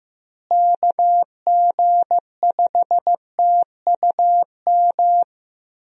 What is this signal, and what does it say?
Morse code, heard as a keyed tone, about 15 wpm, 700 Hz. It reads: KG5TUM
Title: USA Call Signs - 6 Characters - Repeated 3x - 15wpm